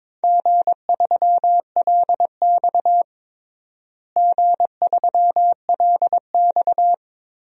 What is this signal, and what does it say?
Morse code, heard as a keyed tone, about 22 wpm, 700 Hz. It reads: G3LX G3LX
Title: DXCC-100 Call Signs - 4 Characters - Repeated 3x - 22wpm